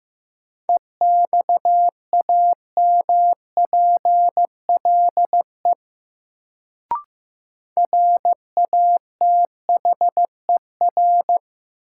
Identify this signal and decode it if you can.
Morse code, heard as a keyed tone, about 16 wpm, 700 Hz. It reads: EXAMPLE RATHER
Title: Sets of 2 Words - New Word Drill - 15wpm